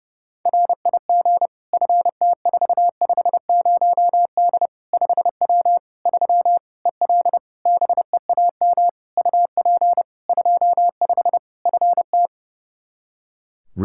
Morse code, heard as a keyed tone, about 30 wpm, 700 Hz. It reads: RIG FT450D 5W 3 EL BEAM UP 25 FT
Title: Practice QSO - Exchange 2 - Info - 30wpm